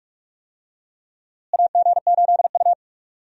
Morse code, heard as a keyed tone, about 46 wpm, 700 Hz. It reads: AG8U
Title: USA Call Signs - 4 Characters - Repeated 3x - 45wpm